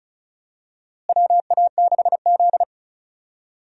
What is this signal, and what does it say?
Morse code, heard as a keyed tone, about 36 wpm, 700 Hz. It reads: WA6Z
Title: USA Call Signs - 4 Characters - Repeated 3x - 35wpm